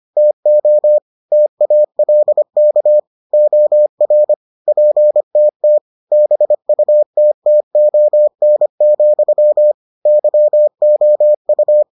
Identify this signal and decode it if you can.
Morse code, heard as a keyed tone, about 25 wpm, 600 Hz. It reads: TO TALK OR PTT BUTTON, YOU